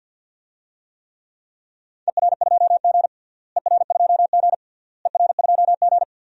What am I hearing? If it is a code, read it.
Morse code, heard as a keyed tone, about 51 wpm, 700 Hz. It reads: ER2G ER2G ER2G
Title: DXCC-100 Call Signs - 4 Characters - Repeated 3x - 50wpm